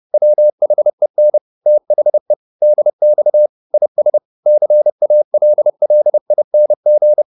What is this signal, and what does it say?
Morse code, heard as a keyed tone, about 30 wpm, 600 Hz. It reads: WHEN THE DX IS CALLING